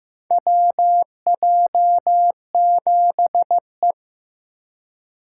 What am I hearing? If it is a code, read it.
Morse code, heard as a keyed tone, about 15 wpm, 700 Hz. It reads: WJ7E
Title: USA Call Signs - 4 Characters - 2x Character Spacing - 15wpm